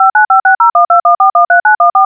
DTMF, heard as telephone touch tones, tones of about 130 ms. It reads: 59560121413914